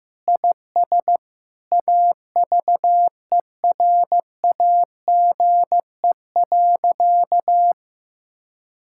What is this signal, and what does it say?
Morse code, heard as a keyed tone, about 16 wpm, 700 Hz. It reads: IS AVERAGE.
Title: Sentences from Top 1000 Words - Review Part 6 - 15wpm